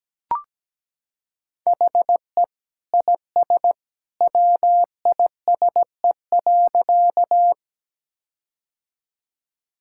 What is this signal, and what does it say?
Morse code, heard as a keyed tone, about 17 wpm, 700 Hz. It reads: HE IS WISE.
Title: Sentences from Top 1600 Words 17wpm